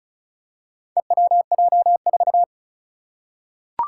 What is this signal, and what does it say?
Morse code, heard as a keyed tone, about 36 wpm, 700 Hz. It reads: EWJ4
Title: Four Challenging Letter-Number Combinations 35wpm